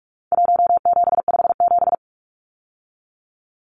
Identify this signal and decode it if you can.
Morse code, heard as a keyed tone, about 44 wpm, 700 Hz. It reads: 1757
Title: Numbers - Contest Style - 4-Digit 45wpm